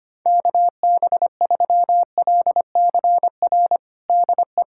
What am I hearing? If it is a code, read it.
Morse code, heard as a keyed tone, about 25 wpm, 700 Hz. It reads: KB3LCR DE
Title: USA Callsign DE Callsign - 6 Characters - 25wpm